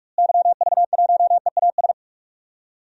Morse code, heard as a keyed tone, about 41 wpm, 700 Hz. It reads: YV1EAS
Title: DXCC-100 Call Signs - 6 Characters - Repeated 2x - 45wpm